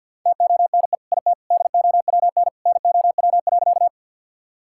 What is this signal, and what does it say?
Morse code, heard as a keyed tone, about 50 wpm, 700 Hz. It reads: TONE IT DOWN NOW.
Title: Sentences from Top 1600 Words 50wpm